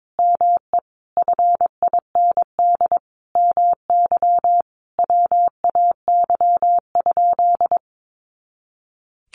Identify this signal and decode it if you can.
Morse code, heard as a keyed tone, about 22 wpm, 700 Hz. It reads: ME FIND MY WAY?
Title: Sentences from Top 200 Words 22wpm